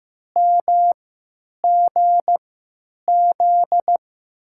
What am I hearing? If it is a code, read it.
Morse code, heard as a keyed tone, about 15 wpm, spaced as for 9 wpm, 700 Hz. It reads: MGZ
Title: USA Call Signs - 6 Characters - 3x Character Spacing - 15wpm